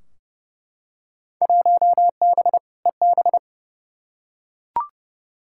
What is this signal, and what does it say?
Morse code, heard as a keyed tone, about 32 wpm, 700 Hz. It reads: 1B EB
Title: Field Day Exchanges Encore - 30wpm